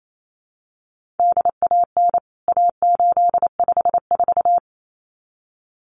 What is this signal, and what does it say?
Morse code, heard as a keyed tone, about 28 wpm, 700 Hz. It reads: DAN A854
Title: CWT Exchange 28wpm